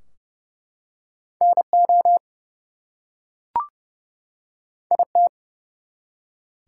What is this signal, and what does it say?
Morse code, heard as a keyed tone, about 31 wpm, 700 Hz. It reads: NO IT
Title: Two Letter Words - 30wpm